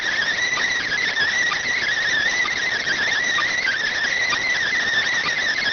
Emergency Alert System (SAME) tones (header bursts), heard as harsh buzzing bursts. It is heard from 0.4 to 4.8 s.